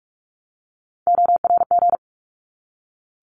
Morse code, heard as a keyed tone, about 44 wpm, 700 Hz. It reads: ORG
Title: ARRL Sections 45wpm